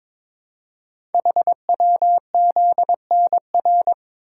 Morse code, heard as a keyed tone, about 22 wpm, 700 Hz. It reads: HWZNR
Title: Code Groups - 5 Letter:Numbers - 22wpm